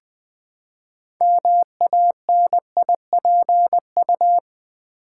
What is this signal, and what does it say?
Morse code, heard as a keyed tone, about 20 wpm, 700 Hz. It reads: MANIPU
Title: Build A Word - 6 Letters At A Time - 12-Letter Words - 20wpm